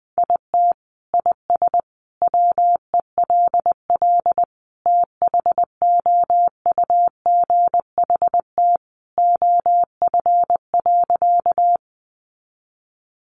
Morse code, heard as a keyed tone, about 20 wpm, 700 Hz. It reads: IT IS WELL THOUGHT OF.
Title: Sentences from Top 300 Words 20wpm